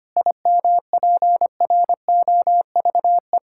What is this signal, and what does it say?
Morse code, heard as a keyed tone, about 25 wpm, 700 Hz. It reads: IMPROVE
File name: practice / 25wpm / Difficult Sentences with Hints - 25wpm